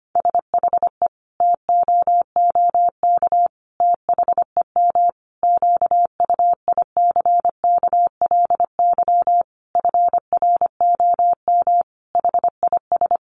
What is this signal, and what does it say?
Morse code, heard as a keyed tone, about 25 wpm, 700 Hz. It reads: SHE TOOK THEM QUICKLY FROM HIS
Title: Sentences from Top 1300 Words 25wpm